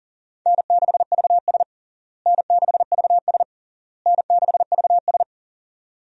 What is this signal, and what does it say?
Morse code, heard as a keyed tone, about 40 wpm, 700 Hz. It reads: N6VS N6VS N6VS
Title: USA Call Signs - 4 Characters - Repeated 3x - 40wpm